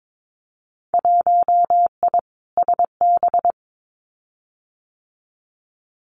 Morse code, heard as a keyed tone, about 22 wpm, 700 Hz. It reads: 1I SB
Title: Winter Field Day Exchanges Encore 22wpm